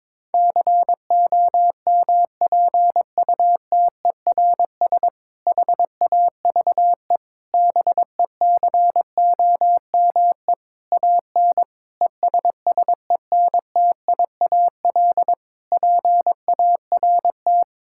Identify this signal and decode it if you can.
Morse code, heard as a keyed tone, about 22 wpm, 700 Hz. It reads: COMPUTERS HAVE BECOME AN ESSENTIAL PART
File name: Difficult Sentences with Hints - 22wpm